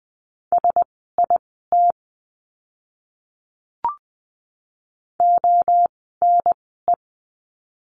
Morse code, heard as a keyed tone, about 20 wpm, spaced as for 15 wpm, 700 Hz. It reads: SIT ONE
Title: 036 Lesson 7f - Letter s - Words 20wpm